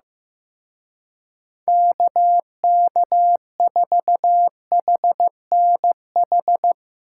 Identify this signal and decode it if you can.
Morse code, heard as a keyed tone, about 16 wpm, 700 Hz. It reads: KK4HNH
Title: Speed Racing - USA 2x3 Call Signs 15wpm